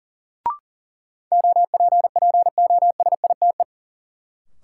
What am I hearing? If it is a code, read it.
Morse code, heard as a keyed tone, about 37 wpm, 700 Hz. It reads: OPPOSITE
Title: Sentences from Top 1300 Words - New Word Drill Alt - 40wpm